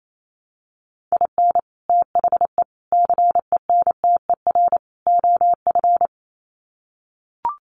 Morse code, heard as a keyed tone, about 28 wpm, 700 Hz. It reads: IN THE CENTER OF
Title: Sets of 4 Words 28wpm